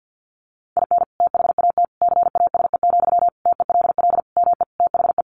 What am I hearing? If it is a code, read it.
Morse code, heard as a keyed tone, about 50 wpm, 700 Hz. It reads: IN THAT CASE, TELL ME THE
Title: Sentences from Top 1400 Words 50wpm